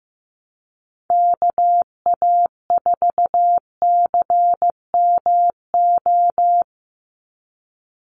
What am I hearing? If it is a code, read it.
Morse code, heard as a keyed tone, about 15 wpm, 700 Hz. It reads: KA4CMO
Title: USA Call Signs 15wpm